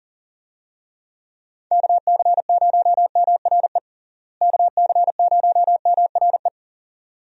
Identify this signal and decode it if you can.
Morse code, heard as a keyed tone, about 40 wpm, 700 Hz. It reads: KC0MRE KC0MRE
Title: USA Call Signs - 6 Characters - Repeated 2x - 40wpm